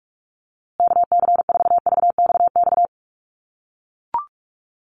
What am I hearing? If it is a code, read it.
Morse code, heard as a keyed tone, about 44 wpm, 700 Hz. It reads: KC4VXX
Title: USA Call Signs Repeated 2x 45wpm